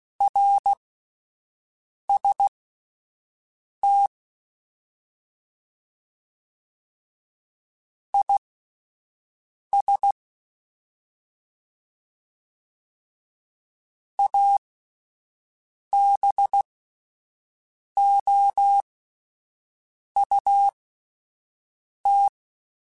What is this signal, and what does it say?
Morse code, heard as a keyed tone, about 16 wpm, spaced as for 5 wpm, 800 Hz. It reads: RST IS ABOUT